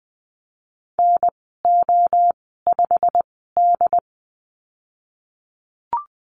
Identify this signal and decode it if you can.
Morse code, heard as a keyed tone, about 20 wpm, spaced as for 15 wpm, 700 Hz. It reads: NO5D
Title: 186 Lesson 30g - Letter b - Call Signs 20wpm